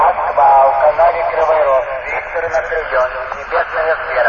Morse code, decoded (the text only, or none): none